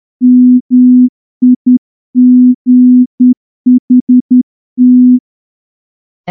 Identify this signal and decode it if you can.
Morse code, heard as a keyed tone, about 10 wpm, 250 Hz. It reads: MIGHT
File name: morse-training-10WPM